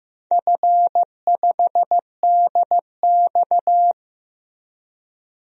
Morse code, heard as a keyed tone, about 15 wpm, 700 Hz. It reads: F5DX
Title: DXCC-100 Call Signs - 4 Characters - Repeated 3x - 15wpm